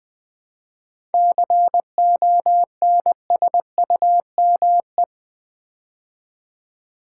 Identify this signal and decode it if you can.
Morse code, heard as a keyed tone, about 20 wpm, 700 Hz. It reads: CONSUME
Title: 7 Letter Words - First 3 Letters Given - 20wpm